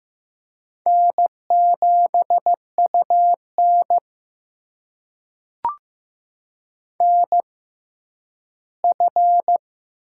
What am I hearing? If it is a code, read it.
Morse code, heard as a keyed tone, about 15 wpm, spaced as for 10 wpm, 700 Hz. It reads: N7UN NF
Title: USA Call Signs - 6x Character Spacing - 15wpm